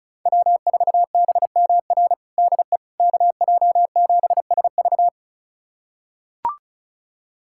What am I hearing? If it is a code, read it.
Morse code, heard as a keyed tone, about 34 wpm, 700 Hz. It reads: W4BMR DE KJ7SV
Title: USA Callsign DE Callsign - 5 Characters - 35wpm